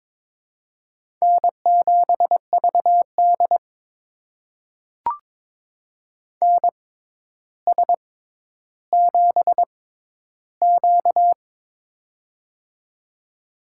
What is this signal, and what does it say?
Morse code, heard as a keyed tone, about 22 wpm, spaced as for 11 wpm, 700 Hz. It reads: N7VD NS7Q
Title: USA Call Signs - 4 Characters - 6x Character Spacing - 22wpm